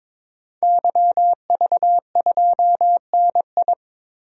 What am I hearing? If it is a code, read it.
Morse code, heard as a keyed tone, about 22 wpm, 700 Hz. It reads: YV2NI
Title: DXCC-100 Call Signs - 5 Characters - Repeated 2x - 22wpm